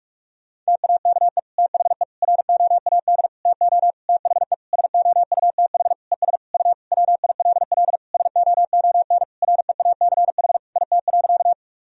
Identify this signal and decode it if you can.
Morse code, heard as a keyed tone, about 46 wpm, 700 Hz. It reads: TAKE THE ROAD TO THE SOUTH ES U WILL SOON REACH IT.
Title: Sentences from Top 400 Words and Abbreviations 45wpm